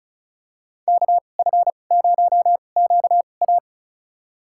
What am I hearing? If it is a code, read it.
Morse code, heard as a keyed tone, about 34 wpm, spaced as for 25 wpm, 700 Hz. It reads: KF0QA
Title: USA Call Signs - 5 Characters - 2x Character Spacing - 35wpm